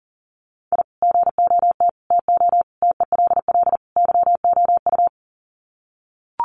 Morse code, heard as a keyed tone, about 40 wpm, 700 Hz. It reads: I GOT TO TELL YOU
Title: Sets of 5 Words Encore2 40wpm